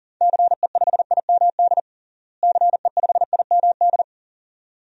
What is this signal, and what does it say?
Morse code, heard as a keyed tone, about 40 wpm, 700 Hz. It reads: CE5IMD CE5IMD
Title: DXCC-100 Call Signs - 6 Characters - Repeated 2x - 40wpm